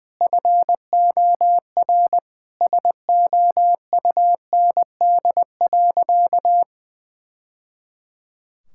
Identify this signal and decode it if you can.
Morse code, heard as a keyed tone, about 21 wpm, 700 Hz. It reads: FOR SOUND.
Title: Sentences from Top 1000 Words - Review Part 6 - 20wpm